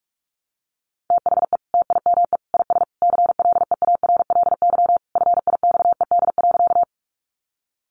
Morse code, heard as a keyed tone, about 44 wpm, 700 Hz. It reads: THE TIME IS CLEARLY FIXED.
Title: Sentences from Top 1300 Words 45wpm